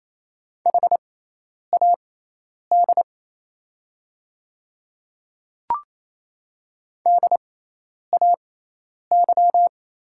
Morse code, heard as a keyed tone, about 28 wpm, spaced as for 10 wpm, 700 Hz. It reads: HAD DAY
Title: 6x Character Spacing - Top 200 Words - 28wpm